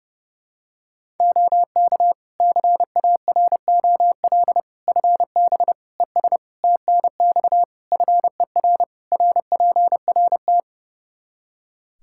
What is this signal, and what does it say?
Morse code, heard as a keyed tone, about 30 wpm, 700 Hz. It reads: OK CAROL FB ES TNX FER RPRT
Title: K1USN SST QSO Practice 30wpm